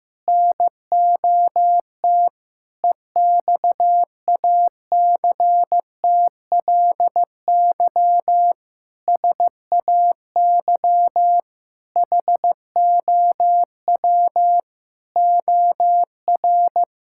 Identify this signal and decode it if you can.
Morse code, heard as a keyed tone, about 14 wpm, 700 Hz. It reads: NOT EXACTLY SAY HOW OR
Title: Sentences from Top 1000 Words - Review Part 6 - 15wpm